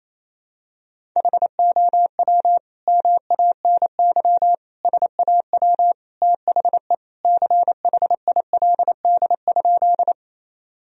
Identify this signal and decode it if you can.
Morse code, heard as a keyed tone, about 28 wpm, 700 Hz. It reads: HOW MANY SAW THE CHILD?
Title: Sentences from Top 1000 Words - Review Part 1 - 28wpm